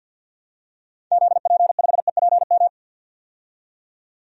Morse code, heard as a keyed tone, about 51 wpm, 700 Hz. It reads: ZP5EPM
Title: DXCC-100 Call Signs - 6 Characters - 50wpm